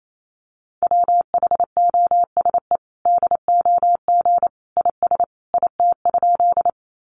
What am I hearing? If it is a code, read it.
Morse code, heard as a keyed tone, about 28 wpm, 700 Hz. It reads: WHOSE DOG IS IT?
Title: Sentences from Top 1200 Words 28wpm